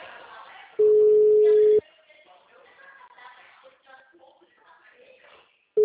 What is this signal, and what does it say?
Ringback tone (Europe, 425 Hz).